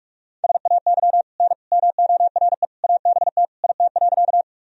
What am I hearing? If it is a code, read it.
Morse code, heard as a keyed tone, about 47 wpm, 700 Hz. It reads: SAY N MORE ABT IT.
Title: Sentences from Top 200 Words and Abbreviations 45wpm